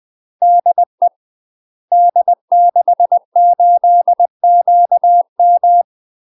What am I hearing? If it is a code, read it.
Morse code, heard as a keyed tone, about 20 wpm, 700 Hz. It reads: DE D68QM
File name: UK MASTER QSOs TRAINING / 20WPM / UK MASTER QSOs 0035